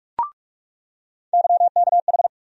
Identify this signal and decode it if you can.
Morse code, heard as a keyed tone, about 45 wpm, 700 Hz. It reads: YKH
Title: USA Call Sign Postfixes - 3 Letters 45wpm